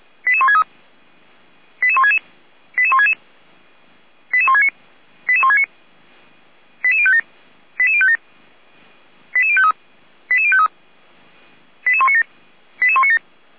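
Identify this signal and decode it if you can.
Five-tone selective call (ZVEI), heard as quick runs of five tones, tones of about 70 ms. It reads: 88163 88166 88166 88169 88169 88057 88057 88053 88053 88197 88197
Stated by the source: ZVEI five-tone call received by SDR.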